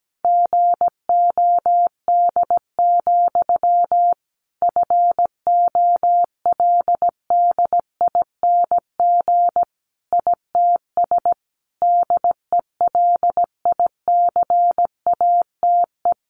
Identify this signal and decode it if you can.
Morse code, heard as a keyed tone, about 17 wpm, 700 Hz. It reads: GOD, FOLDING ITS DELICATE
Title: A Sound Of Thunder 17wpm